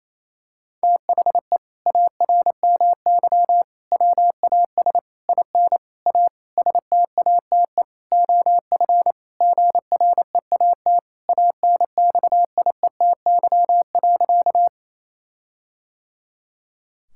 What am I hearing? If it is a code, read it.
Morse code, heard as a keyed tone, about 27 wpm, 700 Hz. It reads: THE ARMY WAS IN A STATE OF GREAT ANXIETY.
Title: Sentences from Top 1600 Words 28wpm